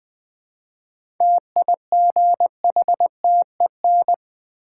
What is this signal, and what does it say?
Morse code, heard as a keyed tone, about 20 wpm, 700 Hz. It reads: TIGHTEN
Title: 7 Letter Words - First 3 Letters Given - 20wpm